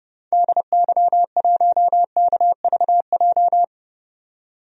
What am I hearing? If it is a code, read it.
Morse code, heard as a keyed tone, about 32 wpm, 700 Hz. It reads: DY1KVJ
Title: DXCC-100 Call Signs - 6 Characters - 30wpm